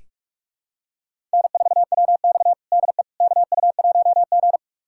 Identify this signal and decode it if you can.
Morse code, heard as a keyed tone, about 46 wpm, 700 Hz. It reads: N4WX DE KU1G
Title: USA Callsign DE Callsign - 4 Characters - 45wpm